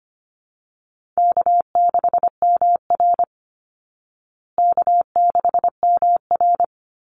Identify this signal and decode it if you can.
Morse code, heard as a keyed tone, about 25 wpm, 700 Hz. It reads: K6MR K6MR
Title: Easy Call Signs Repeated 2x - 25wpm